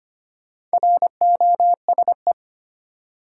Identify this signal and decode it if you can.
Morse code, heard as a keyed tone, about 25 wpm, 700 Hz. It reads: ROSE
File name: practice / 25wpm / Easy Words with Hints - 25wpm